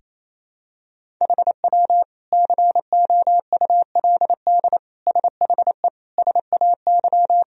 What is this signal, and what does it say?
Morse code, heard as a keyed tone, about 28 wpm, 700 Hz. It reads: HW COULD SHE SAY
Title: Sentences from Top 200 Words and Abbreviations 28wpm